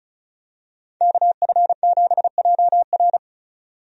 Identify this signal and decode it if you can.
Morse code, heard as a keyed tone, about 36 wpm, 700 Hz. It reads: KF7JR
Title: USA Call Signs - 5 Characters - 35wpm